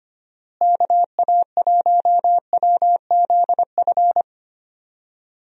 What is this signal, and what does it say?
Morse code, heard as a keyed tone, about 25 wpm, 700 Hz. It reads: KA1WZF
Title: USA Call Signs - 6 Characters - Repeated 3x - 25wpm